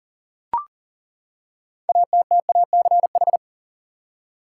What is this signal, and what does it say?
Morse code, heard as a keyed tone, about 40 wpm, 700 Hz. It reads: ATTACH
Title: Six Letter Words - 40wpm